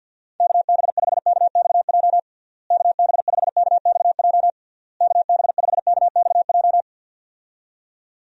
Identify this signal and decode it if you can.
Morse code, heard as a keyed tone, about 49 wpm, 700 Hz. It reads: KB5KXJ KB5KXJ KB5KXJ
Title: USA Call Signs - 6 Characters - Repeated 3x - 50wpm